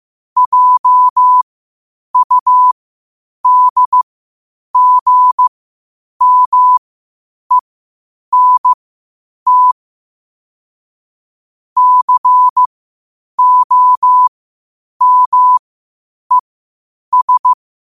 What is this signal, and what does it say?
Morse code, heard as a keyed tone, about 15 wpm, spaced as for 8 wpm, 1000 Hz. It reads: JUDGMENT COMES
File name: qotdfiles / 8wpm